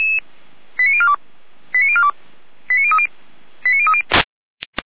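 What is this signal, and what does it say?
Five-tone selective call (ZVEI), heard as quick runs of five tones, tones of about 70 ms. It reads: 89042 89042 89030 89030
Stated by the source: ZVEI five-tone selective call received on an SDR, from the Feuerwehr Orlamünde fire brigade audio server.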